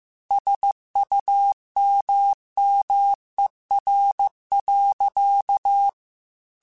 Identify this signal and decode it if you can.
Morse code, heard as a keyed tone, about 15 wpm, 800 Hz. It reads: SUMMER.
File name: QSOs / QSOs-mp3s / WD0BGP 2@13